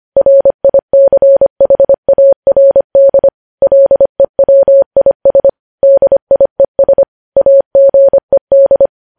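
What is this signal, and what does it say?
Morse code, heard as a keyed tone, about 25 wpm, 550 Hz. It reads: RICHARD LEWIS DIES AGED
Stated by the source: News headlines sent in Morse at 25 wpm.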